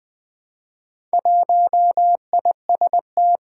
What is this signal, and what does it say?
Morse code, heard as a keyed tone, about 21 wpm, 700 Hz. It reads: 1IST
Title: Four Challenging Letter-Number Combinations 20wpm